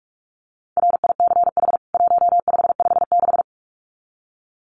Morse code, heard as a keyed tone, about 44 wpm, 700 Hz. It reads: RICH 1556
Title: CWT Exchange 45wpm